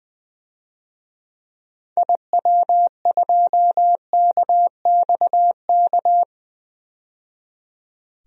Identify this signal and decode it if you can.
Morse code, heard as a keyed tone, about 20 wpm, 700 Hz. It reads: IW2KXK
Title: DXCC-100 Call Signs - 6 Characters - 20wpm